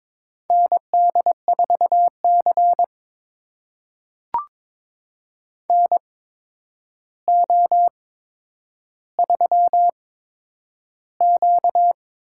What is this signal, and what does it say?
Morse code, heard as a keyed tone, about 22 wpm, spaced as for 9 wpm, 700 Hz. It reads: ND4C NO3Q
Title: USA Call Signs - 4 Characters - 8x Character Spacing - 22wpm